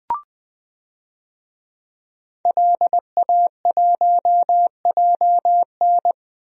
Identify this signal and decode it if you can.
Morse code, heard as a keyed tone, about 20 wpm, 700 Hz. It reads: LA1JN
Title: DXCC-100 Call Signs - 5 Characters - 20wpm